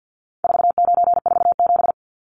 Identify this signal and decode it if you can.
Morse code, heard as a keyed tone, about 50 wpm, 700 Hz. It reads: 4947
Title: Numbers - 4-Digit 50wpm